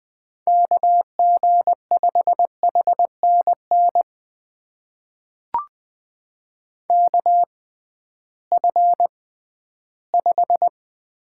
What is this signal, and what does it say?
Morse code, heard as a keyed tone, about 20 wpm, spaced as for 13 wpm, 700 Hz. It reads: KG5HNN KF5
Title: USA Call Signs - 6x Character Spacing - 20wpm